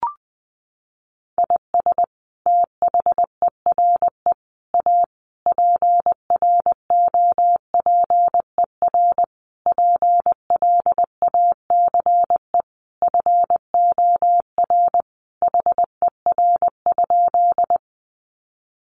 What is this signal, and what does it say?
Morse code, heard as a keyed tone, about 20 wpm, 700 Hz. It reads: IS THERE A PROPER PLACE FOR HER?
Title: Sentences from Top 800 Words 20wpm